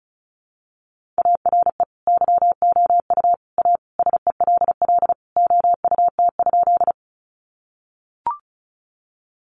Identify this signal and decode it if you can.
Morse code, heard as a keyed tone, about 34 wpm, 700 Hz. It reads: ARE YOU A SELL OUT?
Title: Sentences from Top 1100 Words 35wpm